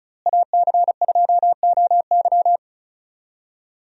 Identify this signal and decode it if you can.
Morse code, heard as a keyed tone, about 34 wpm, 700 Hz. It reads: AC2OY
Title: USA Call Signs - 5 Characters - Repeated 2x - 35wpm